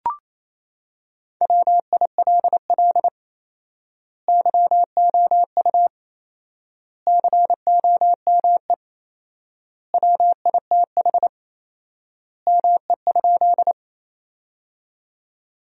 Morse code, heard as a keyed tone, about 28 wpm, 700 Hz. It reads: WILL YOU COME WITH ME?
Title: 4x Word-Spacing - Sentences from Top 200 Words - 28wpm